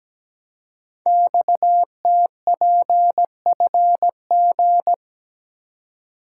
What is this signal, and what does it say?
Morse code, heard as a keyed tone, about 17 wpm, 700 Hz. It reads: XTPFG
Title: Code Groups - 5 Letters - 17wpm